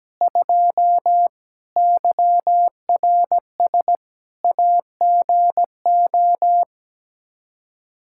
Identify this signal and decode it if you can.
Morse code, heard as a keyed tone, about 17 wpm, 700 Hz. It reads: 2 YRS AGO
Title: Practice QSO - Exchange 3 - More Info - 17wpm